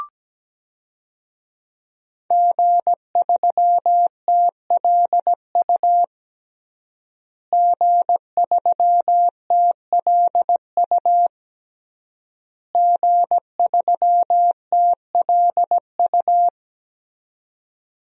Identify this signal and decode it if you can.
Morse code, heard as a keyed tone, about 17 wpm, 700 Hz. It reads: G3TLU G3TLU G3TLU
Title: DXCC-100 Call Signs - 5 Characters - Repeated 3x - 17wpm